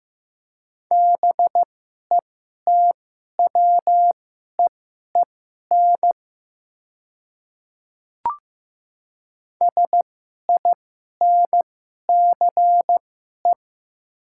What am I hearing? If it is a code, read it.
Morse code, heard as a keyed tone, about 15 wpm, spaced as for 11 wpm, 700 Hz. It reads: BETWEEN SINCE
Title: 2x Character Spacing - Top 500 Words - 15wpm